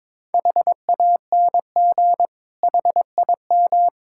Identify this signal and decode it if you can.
Morse code, heard as a keyed tone, about 22 wpm, 700 Hz. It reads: HANG HIM
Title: Sentences from Top 1600 Words 22wpm